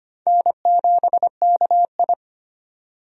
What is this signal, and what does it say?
Morse code, heard as a keyed tone, about 25 wpm, 700 Hz. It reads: N7KI
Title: USA Call Signs - 4 Characters - 25wpm